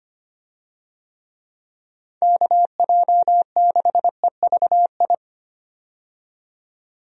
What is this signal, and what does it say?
Morse code, heard as a keyed tone, about 25 wpm, 700 Hz. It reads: KJ6EVI
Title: USA Call Signs - 6 Characters - 25wpm